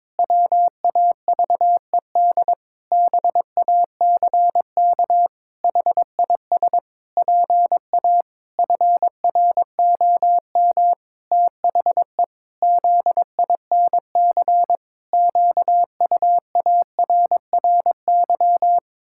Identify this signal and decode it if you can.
Morse code, heard as a keyed tone, about 22 wpm, 700 Hz. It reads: WAVED BACK HIS PA FROM THE ZINC QUARRY
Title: Pangram Challenge v2 - 22wpm